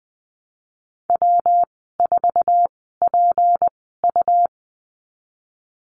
Morse code, heard as a keyed tone, about 20 wpm, spaced as for 15 wpm, 700 Hz. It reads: W4PU
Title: 161 Lesson 26g - Letter g - Call Signs 20wpm